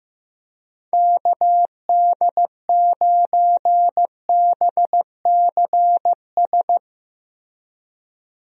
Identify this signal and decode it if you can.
Morse code, heard as a keyed tone, about 15 wpm, 700 Hz. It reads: KD9BCS
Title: USA Call Signs - 6 Characters - Repeated 3x - 15wpm